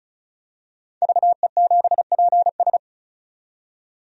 Morse code, heard as a keyed tone, about 36 wpm, 700 Hz. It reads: VE7PS
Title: DXCC-100 Call Signs - 5 Characters - 35wpm